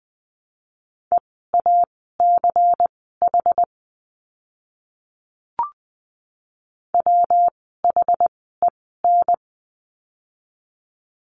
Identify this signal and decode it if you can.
Morse code, heard as a keyed tone, about 20 wpm, spaced as for 14 wpm, 700 Hz. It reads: EACH WHEN
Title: 192 Lesson 31f - Letter v - Words 20wpm